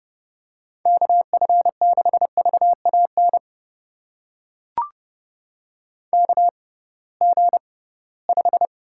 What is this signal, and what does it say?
Morse code, heard as a keyed tone, about 30 wpm, spaced as for 19 wpm, 700 Hz. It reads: KF6VAN KG5
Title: USA Call Signs - 6 Characters - 6x Character Spacing - 30wpm